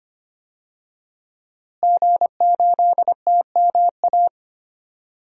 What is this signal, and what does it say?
Morse code, heard as a keyed tone, about 25 wpm, 700 Hz. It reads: G8TMA
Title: DXCC-100 Call Signs - 5 Characters - Repeated 2x - 25wpm